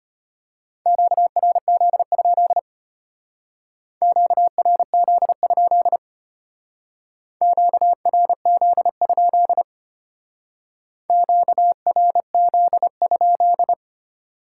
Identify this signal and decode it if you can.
Morse code, heard as a keyed tone, about 32 wpm, 700 Hz. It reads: QRZ? QRZ? QRZ? QRZ?